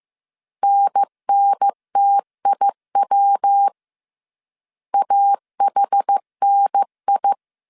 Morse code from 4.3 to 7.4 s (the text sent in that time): AHNI